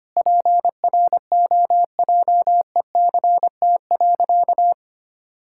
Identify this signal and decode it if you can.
Morse code, heard as a keyed tone, about 25 wpm, 700 Hz. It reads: PROJECT.
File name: practice / 25wpm / Sentences with 9-Word Phrases - 25wpm